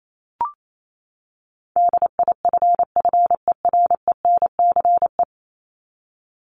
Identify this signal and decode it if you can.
Morse code, heard as a keyed tone, about 28 wpm, 700 Hz. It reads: DIFFERENCE
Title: Top 751-1000 Words 28wpm